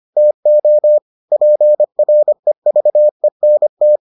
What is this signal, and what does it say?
Morse code, heard as a keyed tone, about 25 wpm, 600 Hz. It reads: TO PREVENT